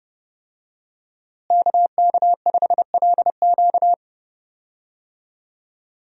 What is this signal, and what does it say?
Morse code, heard as a keyed tone, about 30 wpm, 700 Hz. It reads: KK5LQ
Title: USA Call Signs - 5 Characters - Repeated 2x - 30wpm